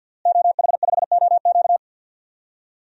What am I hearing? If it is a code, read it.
Morse code, heard as a keyed tone, about 47 wpm, 700 Hz. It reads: OH5OQ
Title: DXCC-100 Call Signs - 5 Characters - Repeated 3x - 50wpm